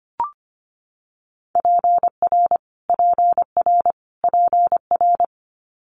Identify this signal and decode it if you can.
Morse code, heard as a keyed tone, about 25 wpm, 700 Hz. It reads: PR PR PR
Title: Multi-letter Phonemes Repeated 3x 25wpm